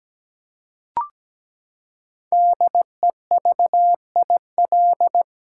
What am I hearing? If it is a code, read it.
Morse code, heard as a keyed tone, about 17 wpm, 700 Hz. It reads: DEVIL
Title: Five Letter Words - 17wpm